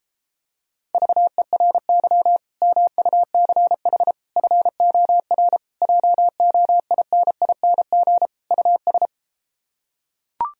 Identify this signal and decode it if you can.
Morse code, heard as a keyed tone, about 33 wpm, 700 Hz. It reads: VERY MUCH FOR JOINING US